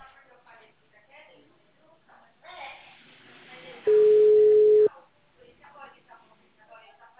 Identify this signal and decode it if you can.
Ringback tone (Europe, 425 Hz), with speech.